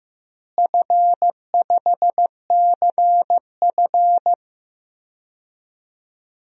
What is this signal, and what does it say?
Morse code, heard as a keyed tone, about 15 wpm, 700 Hz. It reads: F5CF
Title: DXCC-100 Call Signs - 4 Characters - Repeated 3x - 15wpm